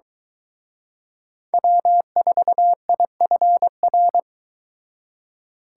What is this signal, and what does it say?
Morse code, heard as a keyed tone, about 23 wpm, 700 Hz. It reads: W4IFR